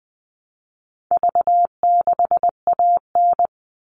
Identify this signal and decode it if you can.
Morse code, heard as a keyed tone, about 20 wpm, 700 Hz. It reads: V6AN
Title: All Cut Numbers - 4-Digit 20wpm